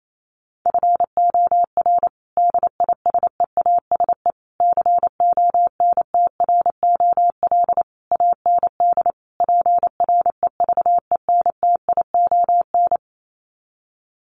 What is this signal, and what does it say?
Morse code, heard as a keyed tone, about 28 wpm, 700 Hz. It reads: FOR DISEASE CONTROL AND PREVENTION
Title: Sets of 5 Words Encore2 28wpm